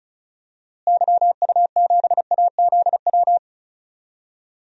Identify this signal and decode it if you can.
Morse code, heard as a keyed tone, about 36 wpm, 700 Hz. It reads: YU7AZW
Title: DXCC-100 Call Signs - 6 Characters - Repeated 3x - 35wpm